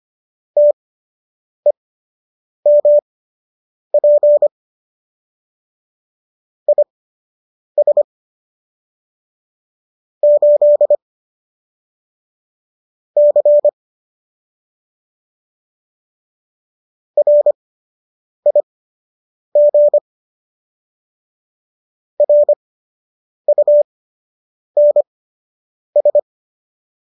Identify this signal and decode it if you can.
Morse code, heard as a keyed tone, about 25 wpm, spaced as for 8 wpm, 590 Hz. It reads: TEMP IS 8 C RIG RUNS
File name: qso003 08